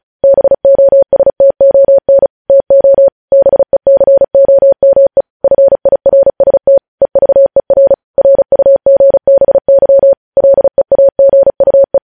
Morse code, heard as a keyed tone, about 36 wpm, 550 Hz. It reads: BOSTON TO BECOME FIRST EVER RUGBY LEAGUE
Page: news headlines in morse code at 35 wpm